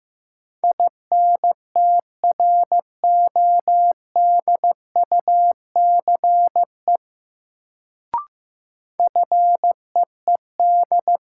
Morse code, heard as a keyed tone, about 15 wpm, 700 Hz. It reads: INTRODUCE FEED
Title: Sentences from Top 1600 Words - New Word Drill - 15wpm